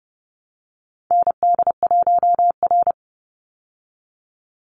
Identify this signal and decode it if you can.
Morse code, heard as a keyed tone, about 30 wpm, 700 Hz. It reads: ND1R
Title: Call Signs Repeated 3x - 30wpm